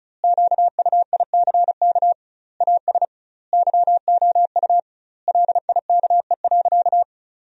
Morse code, heard as a keyed tone, about 36 wpm, 700 Hz. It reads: QUICK AS YOU LIKE.
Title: 2x Word-Spacing - Sentences from Top 500 Words - 35wpm